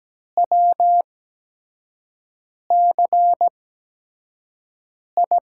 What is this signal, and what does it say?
Morse code, heard as a keyed tone, about 17 wpm, spaced as for 5 wpm, 700 Hz. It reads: WCI
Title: USA Call Signs - 5 Characters - 8x Character Spacing - 17wpm